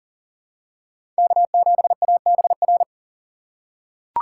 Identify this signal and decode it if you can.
Morse code, heard as a keyed tone, about 40 wpm, 700 Hz. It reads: K7ABR
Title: USA Call Signs - 5 Characters - 3x Character Spacing - 40wpm